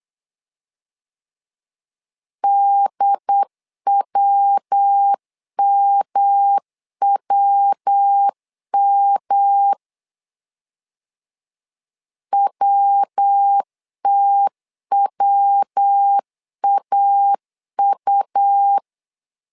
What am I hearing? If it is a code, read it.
Morse code, heard as a keyed tone, about 8 wpm, 800 Hz. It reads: DWMWM WTWAU
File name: LEZIONI / 3 MWRO / 3 MW 8wpm 2501